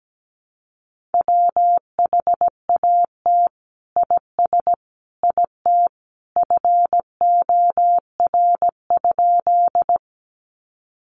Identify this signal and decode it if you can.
Morse code, heard as a keyed tone, about 17 wpm, 700 Hz. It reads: WHAT IS IT FOR?
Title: Sentences from Top 100 Words 17wpm